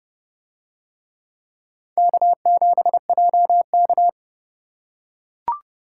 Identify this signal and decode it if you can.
Morse code, heard as a keyed tone, about 29 wpm, 700 Hz. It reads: K7JK